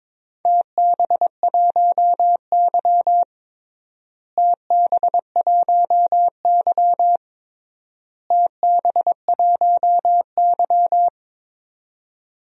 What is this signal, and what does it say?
Morse code, heard as a keyed tone, about 23 wpm, 700 Hz. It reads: TB1Y TB1Y TB1Y
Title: DXCC-100 Call Signs - 4 Characters - Repeated 3x - 22wpm